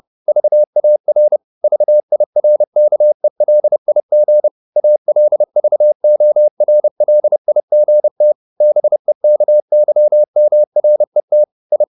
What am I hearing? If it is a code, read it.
Morse code, heard as a keyed tone, about 30 wpm, 600 Hz. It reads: VAR VIRKELIG ALVORLIGT BEKYMRET I
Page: Ferie 30 wpm